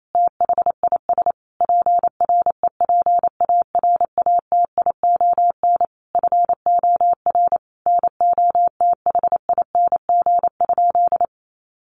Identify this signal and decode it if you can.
Morse code, heard as a keyed tone, about 28 wpm, 700 Hz. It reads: THIS PREPARATION FOR NOTHING?
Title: Sentences from Top 1400 Words 28wpm